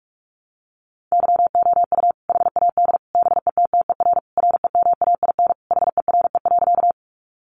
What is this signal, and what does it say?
Morse code, heard as a keyed tone, about 44 wpm, 700 Hz. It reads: YOU HAD BETTER REMAIN HERE.
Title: Sentences from Top 1200 Words 45wpm